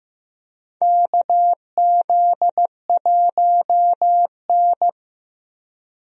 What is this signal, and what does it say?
Morse code, heard as a keyed tone, about 15 wpm, 700 Hz. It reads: KZ1N
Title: USA Call Signs - 4 Characters - Repeated 2x - 15wpm